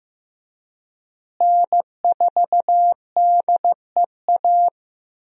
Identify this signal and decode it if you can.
Morse code, heard as a keyed tone, about 15 wpm, 700 Hz. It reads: N4DEA
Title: USA Call Signs - 5 Characters - Repeated 3x - 15wpm